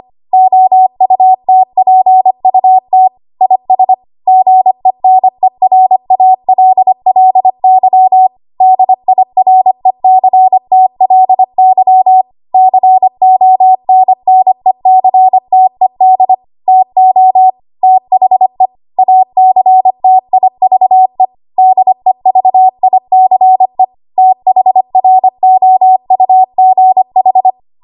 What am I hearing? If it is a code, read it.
Morse code, heard as a keyed tone, about 25 wpm, 750 Hz. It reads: OUTPUT IS GENERALLY DIRECTLY CONNECTED TO THE ACTIVE DEVICE THROUGH
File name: Archive / 25 WPM / 25WPM